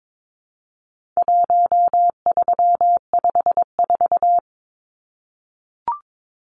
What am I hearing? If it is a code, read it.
Morse code, heard as a keyed tone, about 22 wpm, 700 Hz. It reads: 1354
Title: Numbers - 4-Digit 22wpm